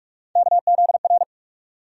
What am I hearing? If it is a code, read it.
Morse code, heard as a keyed tone, about 43 wpm, 700 Hz. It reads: KZR
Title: USA Call Sign Postfixes - 3 Letters 45wpm